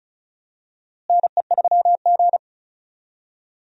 Morse code, heard as a keyed tone, about 34 wpm, 700 Hz. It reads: NE3G
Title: USA Call Signs - 4 Characters - Repeated 3x - 35wpm